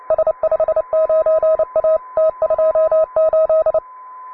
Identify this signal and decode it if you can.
Morse code, heard as a keyed tone, about 29 wpm, 640 Hz. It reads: S59A T28